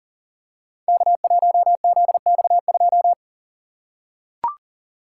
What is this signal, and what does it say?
Morse code, heard as a keyed tone, about 39 wpm, 700 Hz. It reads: K1ZX2
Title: Code Groups - 5 Letter:Numbers - 40wpm